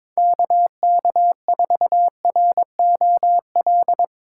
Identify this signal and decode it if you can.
Morse code, heard as a keyed tone, about 23 wpm, 700 Hz. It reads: KK4ROL
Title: Speed Racing - USA 2x3 Call Signs 22wpm